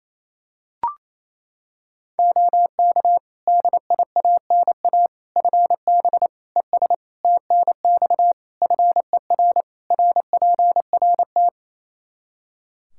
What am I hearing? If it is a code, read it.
Morse code, heard as a keyed tone, about 28 wpm, 700 Hz. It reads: OK DIANA FB ES TNX FER RPRT
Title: Practice QSO - Exchange 2 - Info - 28wpm